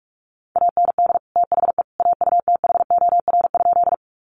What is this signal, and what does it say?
Morse code, heard as a keyed tone, about 44 wpm, 700 Hz. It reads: AND THE AUTHOR?
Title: Sentences from Top 1200 Words 45wpm